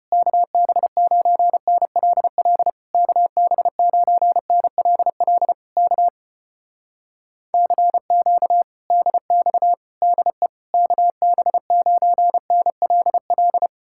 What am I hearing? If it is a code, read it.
Morse code, heard as a keyed tone, about 33 wpm, 700 Hz. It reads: KB9NLL KB9NLL K CQ DX DE KB9NLL